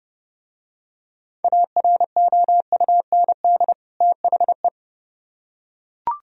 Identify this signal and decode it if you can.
Morse code, heard as a keyed tone, about 28 wpm, 700 Hz. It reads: AROUND THE